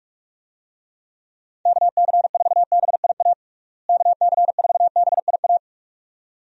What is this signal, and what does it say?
Morse code, heard as a keyed tone, about 44 wpm, 700 Hz. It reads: KC4BIA KC4BIA
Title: USA Call Signs - 6 Characters - Repeated 2x - 45wpm